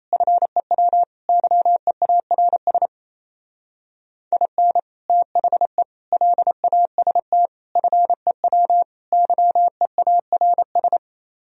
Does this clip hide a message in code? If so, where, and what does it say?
Yes: Morse code from 0.1 to 11.0 s, heard as a keyed tone, about 36 wpm, 700 Hz. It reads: FEW YEARS IN THE LAST FEW YEARS